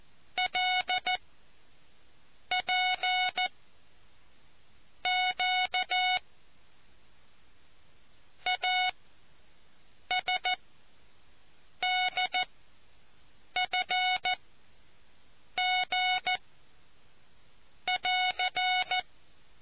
Morse code, heard as a keyed tone, about 14 wpm, spaced as for 5 wpm, 740 Hz. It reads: LPQ ASDFG<AR>